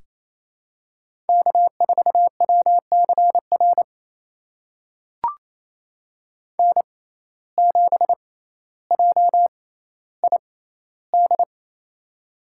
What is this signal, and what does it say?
Morse code, heard as a keyed tone, about 28 wpm, spaced as for 14 wpm, 700 Hz. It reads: K4WCR N7JID
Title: USA Call Signs - 6x Character Spacing - 28wpm